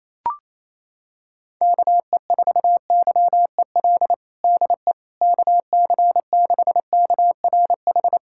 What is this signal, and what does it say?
Morse code, heard as a keyed tone, about 28 wpm, 700 Hz. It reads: KE4YEL DE KC6KRH
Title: USA Callsign DE Callsign 28wpm